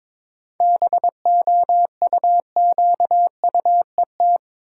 Morse code, heard as a keyed tone, about 22 wpm, 690 Hz. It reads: BOUQUET
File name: Difficult Words with Hints - 22wpm